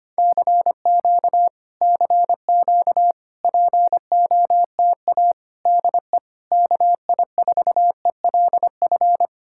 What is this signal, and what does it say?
Morse code, heard as a keyed tone, about 27 wpm, 700 Hz. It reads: CQ CQ POTA DE KI4ELF
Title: Speed-Racing Common QSO Phrases Encore - 25wpm